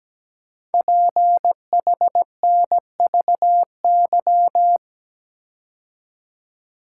Morse code, heard as a keyed tone, about 17 wpm, 700 Hz. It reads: PHNVY
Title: Code Groups - 5 Letters - 17wpm